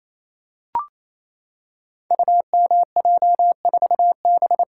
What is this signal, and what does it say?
Morse code, heard as a keyed tone, about 27 wpm, 700 Hz. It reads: UMJ4B
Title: Code Groups - 5 Letter:Numbers - 28wpm